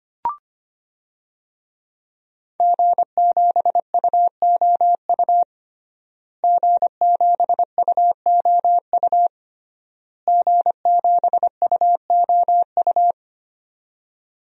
Morse code, heard as a keyed tone, about 25 wpm, 700 Hz. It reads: G7UOU G7UOU G7UOU
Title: DXCC-100 Call Signs - 5 Characters - Repeated 3x - 25wpm